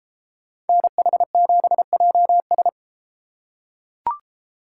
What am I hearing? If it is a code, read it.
Morse code, heard as a keyed tone, about 31 wpm, 700 Hz. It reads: NH7JS